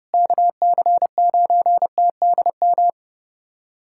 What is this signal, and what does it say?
Morse code, heard as a keyed tone, about 30 wpm, 700 Hz. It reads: KC9TDM
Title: USA Callsign DE Callsign - 6 Characters - 30wpm